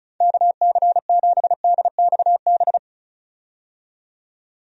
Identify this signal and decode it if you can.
Morse code, heard as a keyed tone, about 36 wpm, 700 Hz. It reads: KC7DXB
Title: USA Call Signs - 6 Characters - 35wpm